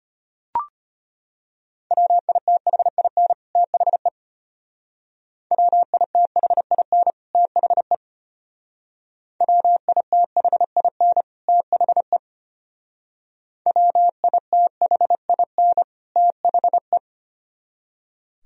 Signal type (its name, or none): Morse code